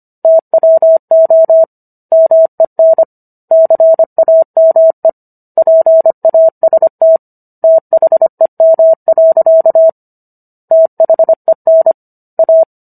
Morse code, heard as a keyed tone, about 25 wpm, 650 Hz. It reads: TWO MEN CAME PAST THEM. THEN A